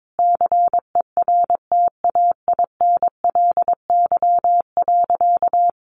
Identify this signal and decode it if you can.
Morse code, heard as a keyed tone, about 22 wpm, 700 Hz. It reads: CERTAINLY.
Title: Sentences from Top 1500 Words 22wpm